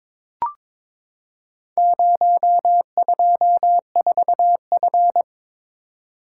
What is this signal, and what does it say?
Morse code, heard as a keyed tone, about 22 wpm, 700 Hz. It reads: 024F
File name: practice / 22wpm / Four Letter-Number Combinations - 22wpm